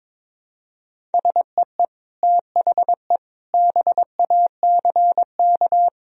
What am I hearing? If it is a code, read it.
Morse code, heard as a keyed tone, about 22 wpm, 700 Hz. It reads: SEE THE BACK
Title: Sentences from Top 1000 Words - Review Part 1 - 22wpm